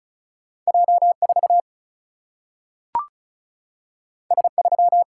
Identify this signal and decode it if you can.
Morse code, heard as a keyed tone, about 36 wpm, 700 Hz. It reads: J4 S3
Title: Single-Letter Single-Number 35wpm